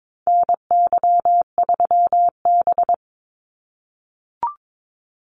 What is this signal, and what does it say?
Morse code, heard as a keyed tone, about 22 wpm, 700 Hz. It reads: NY3B
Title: Call Signs - 22wpm